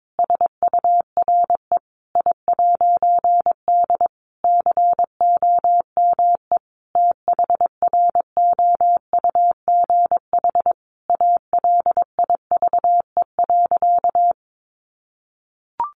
Morse code, heard as a keyed tone, about 22 wpm, 700 Hz. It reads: SURE I'D COME THROUGH ALIVE.
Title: A Sound Of Thunder 22wpm